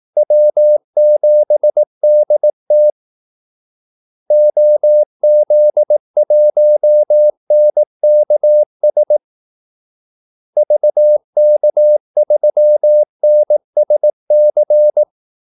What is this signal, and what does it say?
Morse code, heard as a keyed tone, about 18 wpm, 600 Hz. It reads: W7DT OZ1NKS VK3NSC
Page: Kaldesignaler 18 wpm